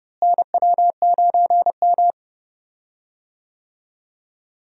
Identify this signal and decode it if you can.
Morse code, heard as a keyed tone, about 29 wpm, 700 Hz. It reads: NW9M
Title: USA Call Signs - 4 Characters - Repeated 3x - 30wpm